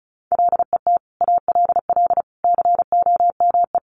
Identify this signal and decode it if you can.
Morse code, heard as a keyed tone, about 34 wpm, 700 Hz. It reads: LET ALL COME
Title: Sentences from Top 1100 Words 35wpm